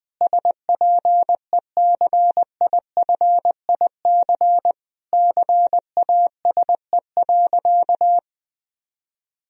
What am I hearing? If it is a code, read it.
Morse code, heard as a keyed tone, about 21 wpm, 700 Hz. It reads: SPECIFIC CASE.
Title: Sentences from Top 1000 Words - Review Part 3 - 20wpm